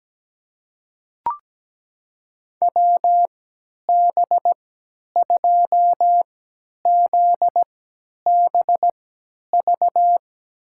Morse code, heard as a keyed tone, about 17 wpm, spaced as for 10 wpm, 700 Hz. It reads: WB2ZBV
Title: USA Call Signs - 6 Characters - 3x Character Spacing - 17wpm